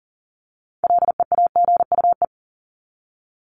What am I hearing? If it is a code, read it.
Morse code, heard as a keyed tone, about 40 wpm, 700 Hz. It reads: LEAGUE
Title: Top 1751-2000 Words 40wpm